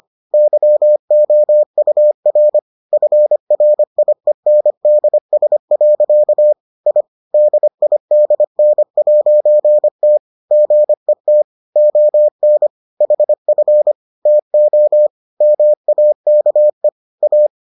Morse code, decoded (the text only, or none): YOUR FRIENDS. I DIDN'T GET ON HF TO MAKE A